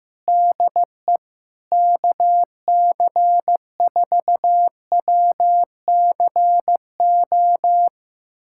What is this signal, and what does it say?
Morse code, heard as a keyed tone, about 15 wpm, 700 Hz. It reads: DE KC4WCO
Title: USA Callsign DE Callsign - 6 Characters - 15wpm